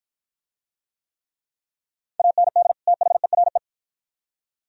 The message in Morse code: AND THERE

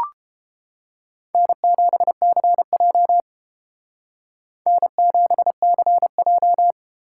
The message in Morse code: N7CJ N7CJ

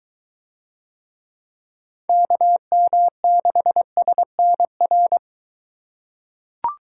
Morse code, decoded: KM6SNR

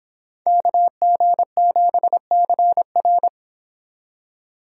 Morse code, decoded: KG7CR

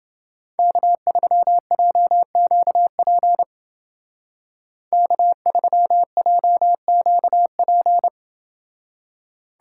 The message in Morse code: K3JQP K3JQP

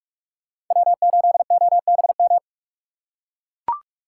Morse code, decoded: W8OBM

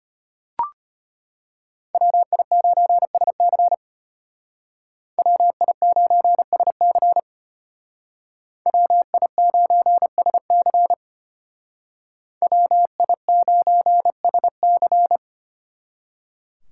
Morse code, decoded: WI9SC WI9SC WI9SC WI9SC

